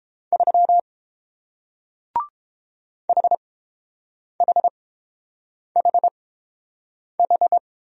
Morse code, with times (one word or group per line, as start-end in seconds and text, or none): 0.3-0.8 3
3.1-7.6 HHHH